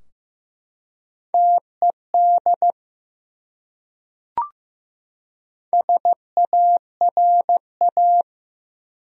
Morse code, heard as a keyed tone, about 16 wpm, 700 Hz. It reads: TED SARA
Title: USA Common Names - 15wpm